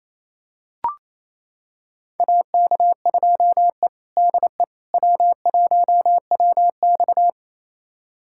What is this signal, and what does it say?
Morse code, heard as a keyed tone, about 29 wpm, 700 Hz. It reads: AK2E DE W1WX
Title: USA Callsign DE Callsign - 4 Characters - 28wpm